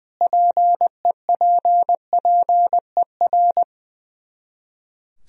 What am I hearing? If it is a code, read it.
Morse code, heard as a keyed tone, about 20 wpm, 700 Hz. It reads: PEPPER
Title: Speed-Racing - Sets of 5 Words 20wpm